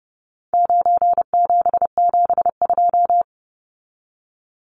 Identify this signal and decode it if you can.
Morse code, heard as a keyed tone, about 30 wpm, 700 Hz. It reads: 9772
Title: Numbers - 4-Digit 30wpm